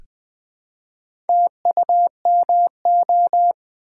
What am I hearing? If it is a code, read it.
Morse code, heard as a keyed tone, about 20 wpm, 700 Hz. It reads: TUMO
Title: Build A Word - 1 Letter At A Time - 5-Letter Words - 20wpm